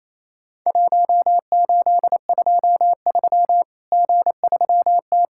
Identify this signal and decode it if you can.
Morse code, heard as a keyed tone, about 28 wpm, 700 Hz. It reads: 1823 G3T
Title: Worked All Europe DX Contest - QTC Exchange - 28wpm